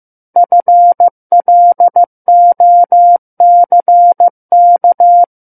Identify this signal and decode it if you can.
Morse code, heard as a keyed tone, about 15 wpm, 700 Hz. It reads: FLOCK
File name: Book One / 06-Chapter 6 - 15 WPM